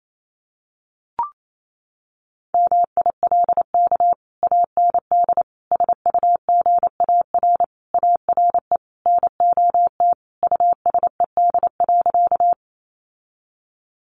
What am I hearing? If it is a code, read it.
Morse code, heard as a keyed tone, about 28 wpm, 700 Hz. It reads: MILK AND SUGAR ARE NOT USED.
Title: Sentences from Top 1500 Words 28wpm